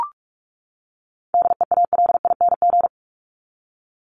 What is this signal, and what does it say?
Morse code, heard as a keyed tone, about 44 wpm, 700 Hz. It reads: DEALING
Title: ING Suffix - Top 100 Words with Base Words 45wpm